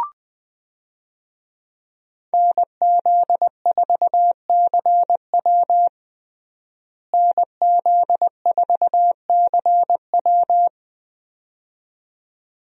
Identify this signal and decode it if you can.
Morse code, heard as a keyed tone, about 20 wpm, 700 Hz. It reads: NZ4CW NZ4CW
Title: USA Call Signs - 5 Characters - Repeated 2x - 20wpm